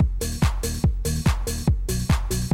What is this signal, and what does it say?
There is no code.